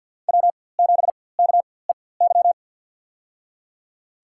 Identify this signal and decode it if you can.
Morse code, heard as a keyed tone, about 46 wpm, spaced as for 23 wpm, 700 Hz. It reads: W7KEY
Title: USA Call Signs - 5 Characters - 4x Character Spacing - 50wpm